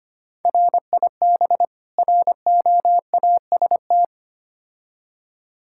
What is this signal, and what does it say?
Morse code, heard as a keyed tone, about 25 wpm, 700 Hz. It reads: RIB ROAST
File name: practice / 25wpm / Difficult Words with Hints - 25wpm